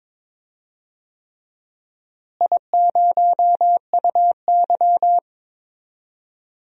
Morse code, heard as a keyed tone, about 22 wpm, 700 Hz. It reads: I0UY
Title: DXCC-100 Call Signs - 4 Characters - Repeated 3x - 22wpm